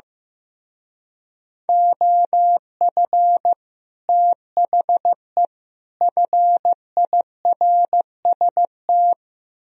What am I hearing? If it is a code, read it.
Morse code, heard as a keyed tone, about 15 wpm, 700 Hz. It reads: OF THE FIRST
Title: Speed-Racing - Sets of 3 Words 15wpm